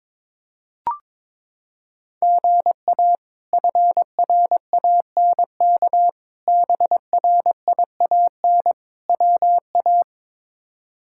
Morse code, heard as a keyed tone, about 23 wpm, 700 Hz. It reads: GA FRANK BRIAN WA
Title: K1USN SST QSO Practice - 22wpm